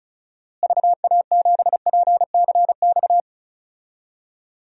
Morse code, heard as a keyed tone, about 34 wpm, 700 Hz. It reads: VA7PCX
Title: DXCC-100 Call Signs - 6 Characters - Repeated 3x - 35wpm